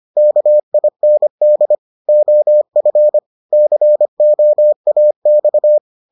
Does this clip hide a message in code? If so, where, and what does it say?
Yes: Morse code from 0.2 to 5.8 s, heard as a keyed tone, about 25 wpm, 600 Hz. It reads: KIND OF COAX